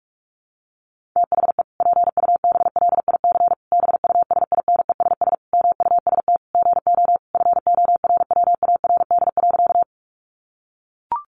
Morse code, heard as a keyed tone, about 44 wpm, 700 Hz. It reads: THE PUBLIC BUSINESS MUST GO FORWARD.
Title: Sentences from Top 1300 Words 45wpm